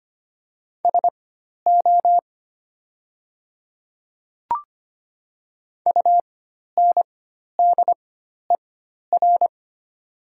Morse code, heard as a keyed tone, about 25 wpm, spaced as for 12 wpm, 700 Hz. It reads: SO UNDER
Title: 4x Character Spacing - Top 300 Words - 25wpm